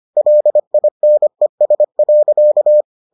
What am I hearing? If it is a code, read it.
Morse code, heard as a keyed tone, about 25 wpm, 600 Hz. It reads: LINES.